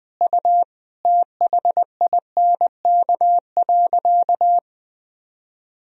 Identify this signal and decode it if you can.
Morse code, heard as a keyed tone, about 20 wpm, 700 Hz. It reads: U THINK.
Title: Sentences from Top 400 Words and Abbreviations 20wpm